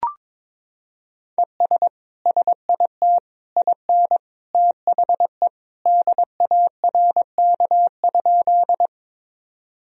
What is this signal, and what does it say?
Morse code, heard as a keyed tone, about 22 wpm, 700 Hz. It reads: ES SIT IN THE DARK?
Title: Sentences from Top 500 Words and Abbreviations 22wpm